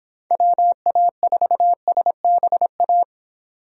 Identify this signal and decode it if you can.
Morse code, heard as a keyed tone, about 26 wpm, 700 Hz. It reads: WA4SBA